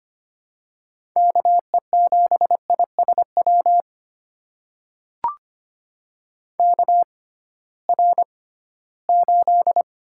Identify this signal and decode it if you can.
Morse code, heard as a keyed tone, about 25 wpm, spaced as for 16 wpm, 700 Hz. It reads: KE7ISW KR8
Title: USA Call Signs - 6 Characters - 6x Character Spacing - 25wpm